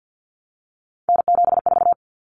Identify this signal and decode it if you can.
Morse code, heard as a keyed tone, about 50 wpm, 700 Hz. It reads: N74
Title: Common Cut Numbers - 3-Digit 50wpm